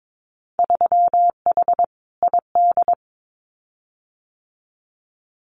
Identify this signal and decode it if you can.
Morse code, heard as a keyed tone, about 22 wpm, 700 Hz. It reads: 3H ID
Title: Winter Field Day Exchanges Encore 22wpm